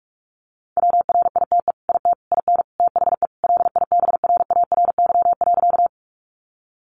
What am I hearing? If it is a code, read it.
Morse code, heard as a keyed tone, about 44 wpm, 700 Hz. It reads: WRITE IT IN THE LIBRARY.
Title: Sentences from Top 600 Words 45wpm